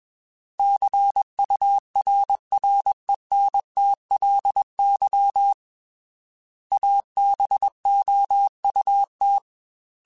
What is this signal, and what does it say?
Morse code, heard as a keyed tone, about 21 wpm, 800 Hz. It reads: CURRENTLY ABOUT